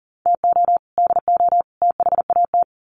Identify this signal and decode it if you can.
Morse code, heard as a keyed tone, about 40 wpm, 700 Hz. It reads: TO DO THAT
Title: Sets of 3 Words 40wpm